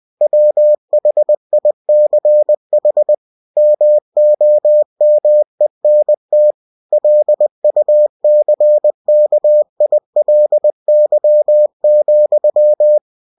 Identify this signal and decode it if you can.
Morse code, heard as a keyed tone, about 20 wpm, 600 Hz. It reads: WHICH MOMENT LUCKILY,
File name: cw-books / winnie-the-pooh / winnie-20wpm-page-0101